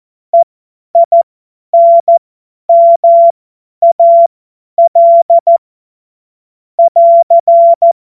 Morse code, heard as a keyed tone, about 14 wpm, spaced as for 10 wpm, 680 Hz. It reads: EINMAL <AR>
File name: Morsekurs / Witze / Witz 28.November 2023 50 F70